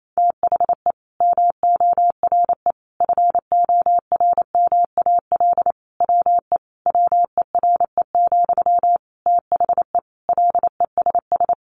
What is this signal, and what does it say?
Morse code, heard as a keyed tone, about 28 wpm, 700 Hz. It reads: THE MORE FORMAL WE WERE, THE LESS
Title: Sentences from Top 1300 Words 28wpm